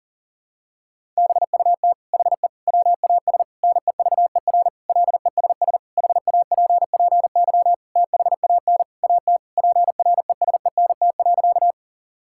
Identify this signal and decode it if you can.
Morse code, heard as a keyed tone, about 40 wpm, 700 Hz. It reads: BUT HE WAS NEVER LESS HAPPY THAN AT PRESENT.
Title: Sentences from Top 1000 Words - Review Part 5 - 40wpm